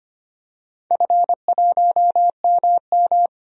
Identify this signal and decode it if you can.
Morse code, heard as a keyed tone, about 25 wpm, 700 Hz. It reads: F1MM
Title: DXCC-100 Call Signs - 4 Characters - 25wpm